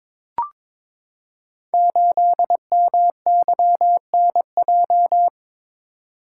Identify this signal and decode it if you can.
Morse code, heard as a keyed tone, about 22 wpm, 690 Hz. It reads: 8MYNJ
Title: Code Groups - 5 Letter:Numbers - 22wpm